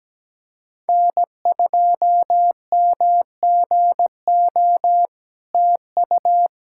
Morse code, heard as a keyed tone, about 17 wpm, 700 Hz. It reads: N2MGO TU
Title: SOTA QSO Practice 17wpm